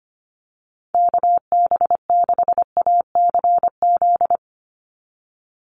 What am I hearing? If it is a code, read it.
Morse code, heard as a keyed tone, about 25 wpm, 700 Hz. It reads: KB6ACZ
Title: USA Call Signs 25wpm